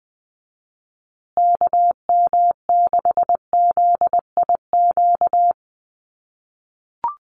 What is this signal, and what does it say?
Morse code, heard as a keyed tone, about 20 wpm, 700 Hz. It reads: KM6ZIQ
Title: USA Call Signs Repeated 2x 20wpm